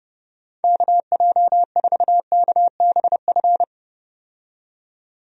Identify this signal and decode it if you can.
Morse code, heard as a keyed tone, about 31 wpm, 700 Hz. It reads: KJ4KBF
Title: USA Call Signs - 6 Characters - 30wpm